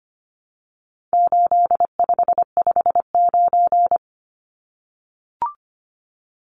Morse code, heard as a keyed tone, about 25 wpm, 700 Hz. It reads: 8559
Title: Numbers - 4-Digit 25wpm